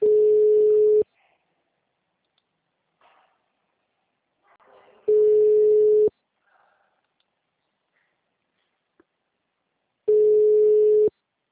Ringback tone (Europe, 425 Hz).
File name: out- -2003- - - .27428